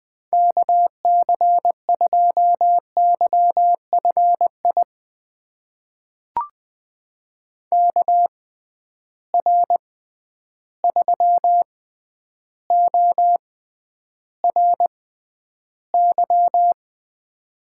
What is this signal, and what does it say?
Morse code, heard as a keyed tone, about 20 wpm, spaced as for 10 wpm, 700 Hz. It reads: KC2YFI KR3ORY
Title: USA Call Signs - 6 Characters - 6x Character Spacing - 20wpm